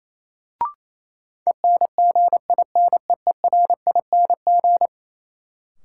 Morse code, heard as a keyed tone, about 28 wpm, 700 Hz. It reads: ENGINEERING
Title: Sentences from Top 1500 Words - New Word Drill - 28wpm